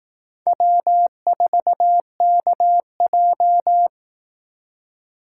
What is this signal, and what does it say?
Morse code, heard as a keyed tone, about 18 wpm, 700 Hz. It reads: W4KJ
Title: Speed Racing - USA 1x2 Call Signs 15wpm